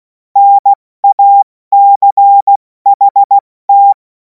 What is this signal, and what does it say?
Morse code, heard as a keyed tone, about 16 wpm, 800 Hz. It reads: NACHT
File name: Morsekurs / Witze / Witz 07. Juli 2026 70 F80